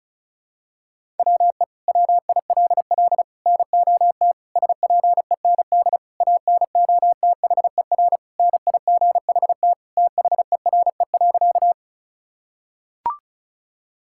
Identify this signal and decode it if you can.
Morse code, heard as a keyed tone, about 34 wpm, 700 Hz. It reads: WE WILL NOT SPEND ANOTHER NIGHT THERE.
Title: Sentences from Top 1000 Words - Review Part 5 - 35wpm